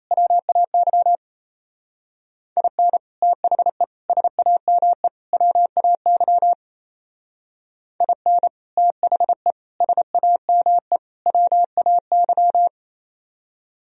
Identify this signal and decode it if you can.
Morse code, heard as a keyed tone, about 35 wpm, 700 Hz. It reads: WAY IN THE SAME WAY IN THE SAME WAY
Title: Speed-Racing - Sets of 4 Words 28wpm